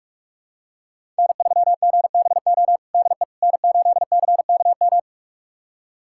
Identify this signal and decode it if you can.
Morse code, heard as a keyed tone, about 46 wpm, 700 Hz. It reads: N3GBO DE N8CKM
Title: USA Callsign DE Callsign - 5 Characters - 45wpm